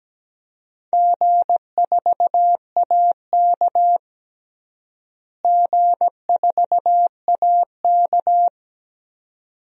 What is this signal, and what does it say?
Morse code, heard as a keyed tone, about 17 wpm, 700 Hz. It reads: G4AK G4AK
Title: DXCC-100 Call Signs - 4 Characters - Repeated 2x - 17wpm